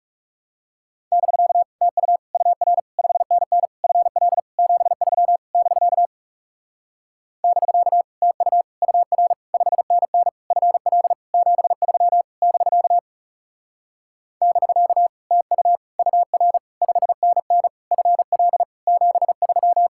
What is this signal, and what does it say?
Morse code, heard as a keyed tone, about 40 wpm, 700 Hz. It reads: <BK> TU UR 5NN FL 73 <BK> <BK> TU UR 5NN FL 73 <BK> <BK> TU UR 5NN FL 73